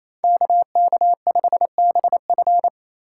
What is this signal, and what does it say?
Morse code, heard as a keyed tone, about 28 wpm, 700 Hz. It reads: KK5BF
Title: USA Call Signs - 5 Characters - 4x Character Spacing - 28wpm